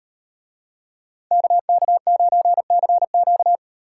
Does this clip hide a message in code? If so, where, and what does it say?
Yes: Morse code from 1.3 to 3.6 s, heard as a keyed tone, about 38 wpm, 700 Hz. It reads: KK9CQ